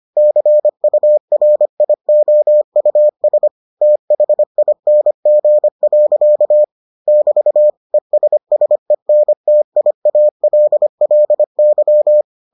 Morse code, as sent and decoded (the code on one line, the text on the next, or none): -.-. ..- .-. .. --- ..- ... / - .... .. -. --. .-.-.- / -...- / . ... ... -. - .. .- .-.. .-.. -.--
CURIOUS THING. <BT> ESSENTIALLY